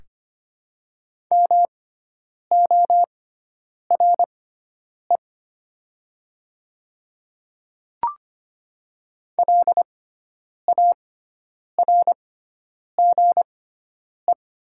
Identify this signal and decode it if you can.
Morse code, heard as a keyed tone, about 25 wpm, spaced as for 9 wpm, 690 Hz. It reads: MORE LARGE
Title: 6x Character Spacing - Top 300 Words - 25wpm